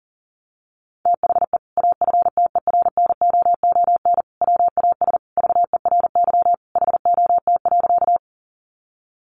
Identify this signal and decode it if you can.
Morse code, heard as a keyed tone, about 37 wpm, 700 Hz. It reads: THE AFTERNOON WAS VERY HOT.
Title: Sentences from Top 1500 Words 40wpm